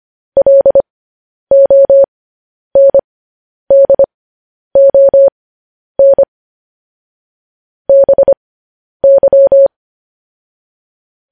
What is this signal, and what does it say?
Morse code, heard as a keyed tone, about 25 wpm, spaced as for 10 wpm, 550 Hz. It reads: LONDON BY
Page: news headlines in morse code at 10 wpm